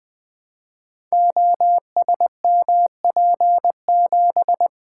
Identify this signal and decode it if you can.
Morse code, heard as a keyed tone, about 21 wpm, 700 Hz. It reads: OSMP7
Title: Code Groups - 5 Letter:Numbers - 20wpm